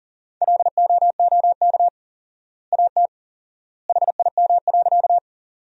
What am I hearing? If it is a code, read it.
Morse code, heard as a keyed tone, about 40 wpm, 700 Hz. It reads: LOOK AT HIM.
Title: 4x Word-Spacing - Sentences from Top 100 Words - 40wpm